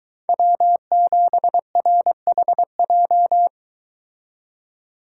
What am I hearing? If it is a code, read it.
Morse code, heard as a keyed tone, about 22 wpm, 700 Hz. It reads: W7RHJ